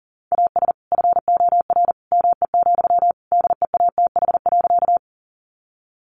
Morse code, heard as a keyed tone, about 40 wpm, 700 Hz. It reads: AS FOR ME, DEATH.
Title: Sentences from Top 1100 Words 40wpm